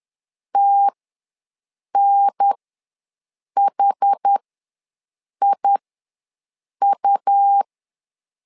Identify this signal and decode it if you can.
Morse code, heard as a keyed tone, about 10 wpm, spaced as for 6 wpm, 800 Hz. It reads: TNHIU
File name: LEZIONI / 2 UHDN / 2 UHDN ws 10wpm 2044